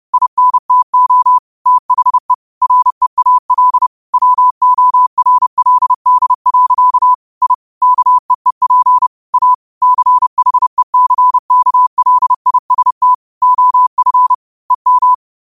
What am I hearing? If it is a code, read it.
Morse code, heard as a keyed tone, about 30 wpm, 1000 Hz. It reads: INTO THE REAL WORLD. I KEEP A CHECKLIST OF EM
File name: qotdfiles / 30wpm